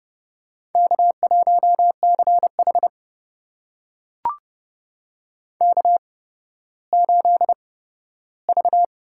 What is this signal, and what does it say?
Morse code, heard as a keyed tone, about 31 wpm, spaced as for 15 wpm, 700 Hz. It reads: K1CH K8V
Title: USA Call Signs - 4 Characters - 8x Character Spacing - 30wpm